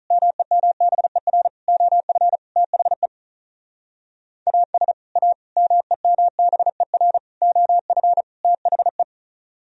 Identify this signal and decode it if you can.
Morse code, heard as a keyed tone, about 46 wpm, 700 Hz. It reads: MEMBER OF THE AS A MEMBER OF THE